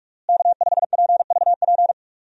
Morse code, heard as a keyed tone, about 41 wpm, 700 Hz. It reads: K5PVP
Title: USA Call Signs - 5 Characters - Repeated 3x - 45wpm